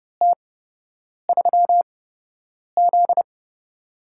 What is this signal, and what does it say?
Morse code, heard as a keyed tone, about 30 wpm, spaced as for 8 wpm, 700 Hz. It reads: T3Z
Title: USA Call Signs - 4 Characters - 8x Character Spacing - 30wpm